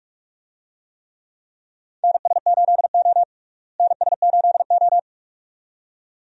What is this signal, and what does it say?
Morse code, heard as a keyed tone, about 42 wpm, 700 Hz. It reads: NS8O NS8O
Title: USA Call Signs - 4 Characters - Repeated 2x - 45wpm